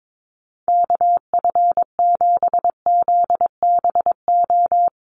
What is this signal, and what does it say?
Morse code, heard as a keyed tone, about 22 wpm, 700 Hz. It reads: KF7ZBO
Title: USA Call Signs Repeated 3x 22wpm